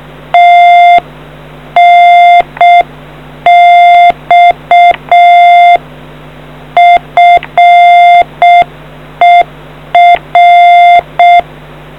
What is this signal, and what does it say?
Morse code, heard as a keyed tone, about 6 wpm, 700 Hz. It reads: TNX FER